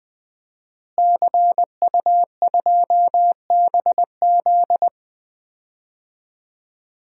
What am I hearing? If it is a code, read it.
Morse code, heard as a keyed tone, about 20 wpm, 700 Hz. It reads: CU2BZ
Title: DXCC-100 Call Signs - 5 Characters - Repeated 2x - 20wpm